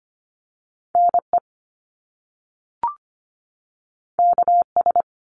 Morse code, heard as a keyed tone, about 25 wpm, 700 Hz. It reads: NE KS
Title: US States 25wpm